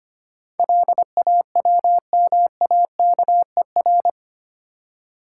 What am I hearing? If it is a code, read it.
Morse code, heard as a keyed tone, about 25 wpm, 700 Hz. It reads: LAWMAKER
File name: Eight Letter Words - 25wpm